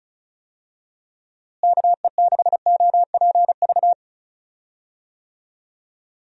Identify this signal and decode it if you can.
Morse code, heard as a keyed tone, about 34 wpm, 700 Hz. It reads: KE6OPV
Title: USA Call Signs - 6 Characters - Repeated 2x - 35wpm